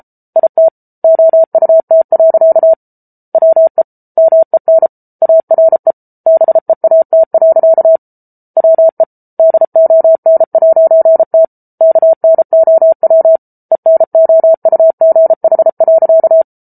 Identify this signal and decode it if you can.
Morse code, heard as a keyed tone, about 34 wpm, 650 Hz. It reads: IT OUT. WE MEN ARE BEAT. WE DON'T KNOW ENOUGH.